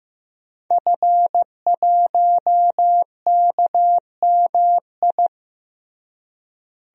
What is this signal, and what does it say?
Morse code, heard as a keyed tone, about 15 wpm, 700 Hz. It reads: F1KMI
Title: DXCC-100 Call Signs - 5 Characters - Repeated 2x - 15wpm